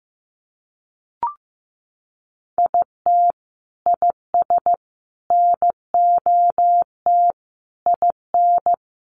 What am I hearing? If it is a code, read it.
Morse code, heard as a keyed tone, about 15 wpm, 700 Hz. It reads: IT IS NOT IN
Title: Sentences from Top 1200 Words 15wpm